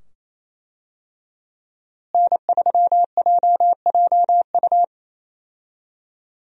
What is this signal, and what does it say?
Morse code, heard as a keyed tone, about 28 wpm, 700 Hz. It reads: N3JJU
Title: USA Call Signs - 5 Characters - Repeated 2x - 28wpm